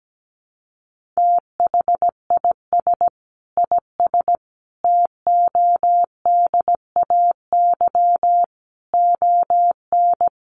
Morse code, heard as a keyed tone, about 17 wpm, 700 Hz. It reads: THIS IS TODAY ON
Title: Sets of 5 Words Encore 17wpm